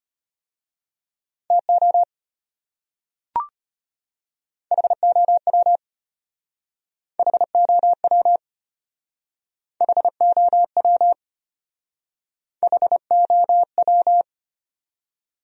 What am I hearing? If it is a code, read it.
Morse code, heard as a keyed tone, about 32 wpm, 700 Hz. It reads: TO HOW HOW HOW HOW